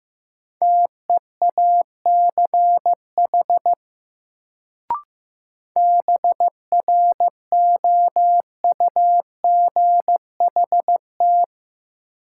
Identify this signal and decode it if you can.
Morse code, heard as a keyed tone, about 16 wpm, 700 Hz. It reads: TEACH BROUGHT
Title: Sentences from Top 500 Words - New Word Drill - 15wpm